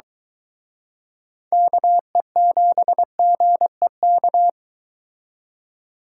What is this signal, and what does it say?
Morse code, heard as a keyed tone, about 22 wpm, 700 Hz. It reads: KE7GEK